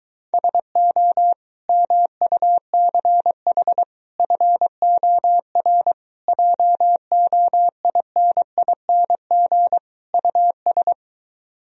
Morse code, heard as a keyed tone, about 23 wpm, 700 Hz. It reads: SO MUCH FOR JOINING US